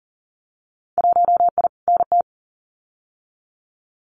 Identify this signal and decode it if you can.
Morse code, heard as a keyed tone, about 40 wpm, 700 Hz. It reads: 1I NT
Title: Winter Field Day Exchanges Encore 40wpm